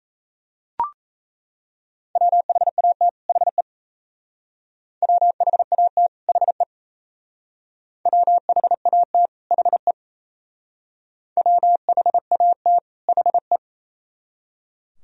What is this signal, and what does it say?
Morse code, heard as a keyed tone, about 37 wpm, 700 Hz. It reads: WHAT HE WHAT HE WHAT HE WHAT HE